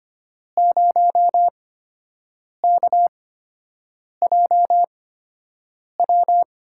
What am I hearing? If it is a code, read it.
Morse code, heard as a keyed tone, about 25 wpm, spaced as for 7 wpm, 700 Hz. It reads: 0KJW
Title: USA Call Signs - 8x Character Spacing - 25wpm